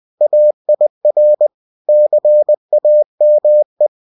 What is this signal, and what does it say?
Morse code, heard as a keyed tone, about 20 wpm, 600 Hz. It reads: AIR CAME